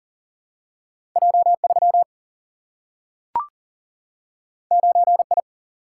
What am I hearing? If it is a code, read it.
Morse code, heard as a keyed tone, about 40 wpm, 700 Hz. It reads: J 3 9 I
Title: Single-Letter Single-Number 40wpm